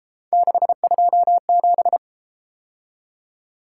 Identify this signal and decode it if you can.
Morse code, heard as a keyed tone, about 32 wpm, 700 Hz. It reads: 627